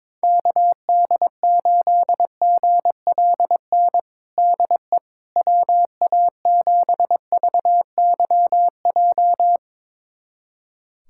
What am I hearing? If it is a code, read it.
Morse code, heard as a keyed tone, about 22 wpm, 700 Hz. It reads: KD8GLN DE WA7VYJ
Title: USA Callsign DE Callsign 22wpm